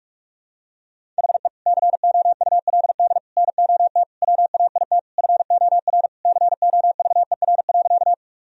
Morse code, heard as a keyed tone, about 48 wpm, 700 Hz. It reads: HE COULD NOT WAIT FOR COVER.
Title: Sentences from Top 1000 Words - Review Part 1 - 45wpm